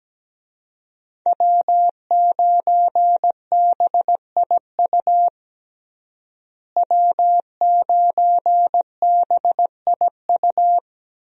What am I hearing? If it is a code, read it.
Morse code, heard as a keyed tone, about 17 wpm, 700 Hz. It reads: W9BIU W9BIU
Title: USA Call Signs - 5 Characters - Repeated 3x - 17wpm